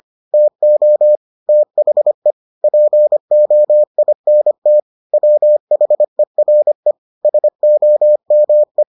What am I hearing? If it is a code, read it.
Morse code, heard as a keyed tone, about 25 wpm, 600 Hz. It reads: TO THE POINT WHERE SOME